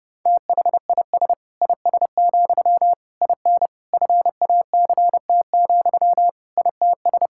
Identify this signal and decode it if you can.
Morse code, heard as a keyed tone, about 30 wpm, 700 Hz. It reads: THIS IS, IN FACT, ITS
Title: Sentences from Top 1000 Words - Review Part 5 - 30wpm